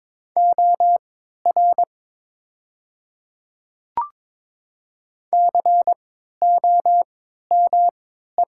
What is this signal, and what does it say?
Morse code, heard as a keyed tone, about 22 wpm, spaced as for 13 wpm, 690 Hz. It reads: OR COME
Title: 3x Character Spacing - Top 100 Words - 22wpm